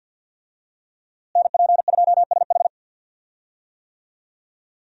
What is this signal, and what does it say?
Morse code, heard as a keyed tone, about 47 wpm, 700 Hz. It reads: NP2SH
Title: USA Call Signs - 5 Characters - Repeated 2x - 50wpm